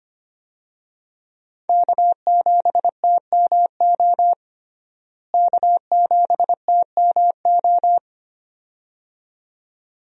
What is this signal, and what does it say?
Morse code, heard as a keyed tone, about 25 wpm, 700 Hz. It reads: K7TMO K7TMO
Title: USA Call Signs - 5 Characters - Repeated 2x - 25wpm